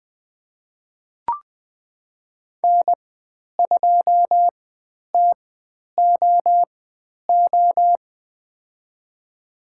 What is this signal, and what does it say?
Morse code, heard as a keyed tone, about 20 wpm, spaced as for 10 wpm, 700 Hz. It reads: N2TOO
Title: 095 Lesson 16g - Letter c - Call Signs 20wpm